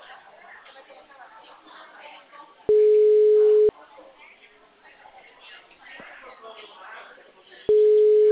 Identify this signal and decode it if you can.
Ringback tone (Europe, 425 Hz), with speech.